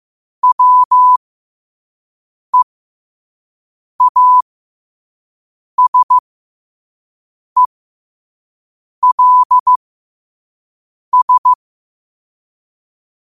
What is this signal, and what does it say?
Morse code, heard as a keyed tone, about 15 wpm, spaced as for 5 wpm, 1000 Hz. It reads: WEASELS
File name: qotdfiles / 5wpm